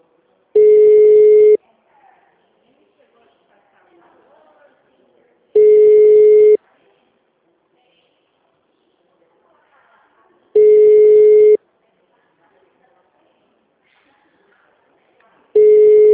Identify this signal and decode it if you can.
Ringback tone (Europe, 425 Hz), with speech.
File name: out- -2002- - - .62260